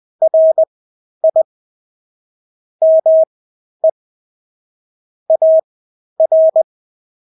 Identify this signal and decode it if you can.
Morse code, heard as a keyed tone, about 20 wpm, spaced as for 11 wpm, 640 Hz. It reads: RI ME AR
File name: State-Abbreviations-10